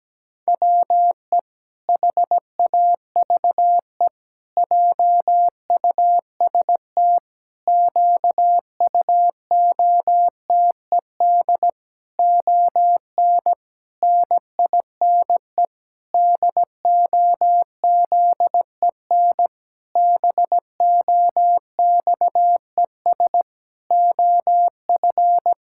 Morse code, heard as a keyed tone, about 17 wpm, 700 Hz. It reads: WE HAVE JUST QUOTED ON NINE DOZEN BOXES OF
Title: Pangram Challenge v2 - 17wpm